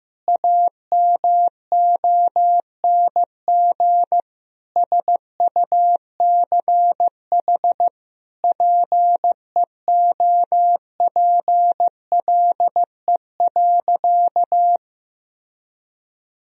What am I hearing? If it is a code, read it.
Morse code, heard as a keyed tone, about 15 wpm, 700 Hz. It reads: AMONG SUCH PEOPLE.
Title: Sentences from Top 1000 Words - Review Part 3 - 15wpm